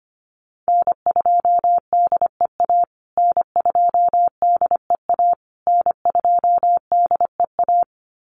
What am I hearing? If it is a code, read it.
Morse code, heard as a keyed tone, about 25 wpm, 700 Hz. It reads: N2DEA N2DEA N2DEA
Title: USA Call Signs Repeated 3x 25wpm